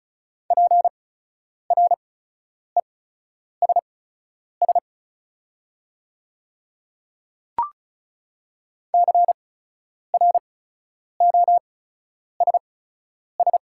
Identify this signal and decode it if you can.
Morse code, heard as a keyed tone, about 34 wpm, spaced as for 10 wpm, 700 Hz. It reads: PRESS CROSS
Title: 8x Character Spacing - Top 500 Words - 35wpm